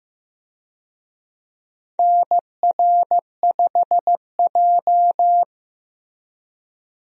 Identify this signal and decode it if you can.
Morse code, heard as a keyed tone, about 15 wpm, 700 Hz. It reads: NR5J
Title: USA Call Signs - 4 Characters - Repeated 2x - 15wpm